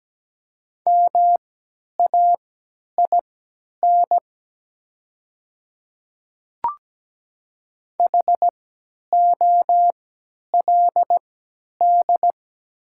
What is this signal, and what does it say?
Morse code, heard as a keyed tone, about 17 wpm, spaced as for 10 wpm, 700 Hz. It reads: MAIN HOLD
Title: 3x Character Spacing - Top 500 Words - 17wpm